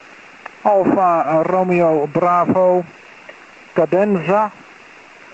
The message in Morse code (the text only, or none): none